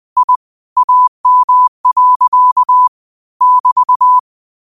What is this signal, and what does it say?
Morse code, heard as a keyed tone, about 20 wpm, 1000 Hz. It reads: I AM. <BT>